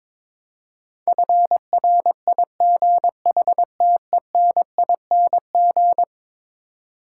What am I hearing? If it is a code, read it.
Morse code, heard as a keyed tone, about 21 wpm, 700 Hz. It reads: FRIGHTENING
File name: practice / 22wpm / Eleven Letter Words - 22wpm